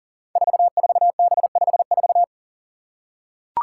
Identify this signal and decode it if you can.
Morse code, heard as a keyed tone, about 38 wpm, 700 Hz. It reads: 44B54
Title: Five Challenging Letter-Number Combinations 40wpm